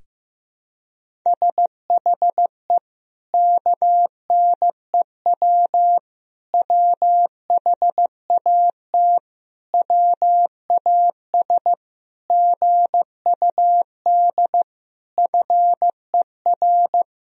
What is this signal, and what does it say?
Morse code, heard as a keyed tone, about 15 wpm, 700 Hz. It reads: SHE KNEW WHAT WAS GUD FER
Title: Sentences from Top 400 Words and Abbreviations 15wpm